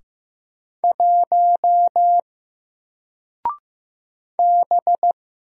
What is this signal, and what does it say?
Morse code, heard as a keyed tone, about 15 wpm, 700 Hz. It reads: 1 B
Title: Single Character (Letter:Number:Punctuation:Prosign) Alt 15wpm